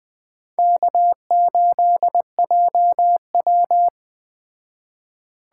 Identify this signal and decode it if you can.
Morse code, heard as a keyed tone, about 19 wpm, 700 Hz. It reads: K8JW